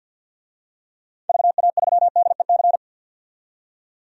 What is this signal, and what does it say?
Morse code, heard as a keyed tone, about 48 wpm, 700 Hz. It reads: VA3DEC
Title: DXCC-100 Call Signs - 6 Characters - 50wpm